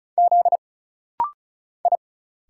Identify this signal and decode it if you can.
Morse code, heard as a keyed tone, about 35 wpm, 700 Hz. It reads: Z I
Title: Single Letters - Rapid-Fire 35wpm